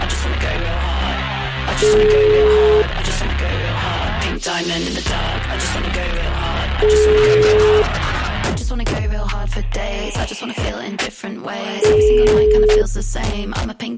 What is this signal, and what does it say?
Ringback tone (Europe, 425 Hz), with speech.